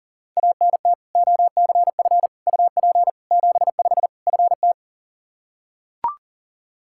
Morse code, heard as a keyed tone, about 37 wpm, 700 Hz. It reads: ANT OCF UP 75 FT
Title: Practice QSO - Exchange 2 - Info - 40wpm